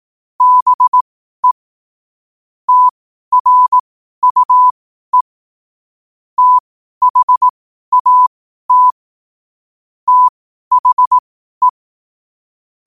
Morse code, heard as a keyed tone, about 18 wpm, spaced as for 12 wpm, 1000 Hz. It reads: BE TRUE THAT THE
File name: qotdfiles / 12wpm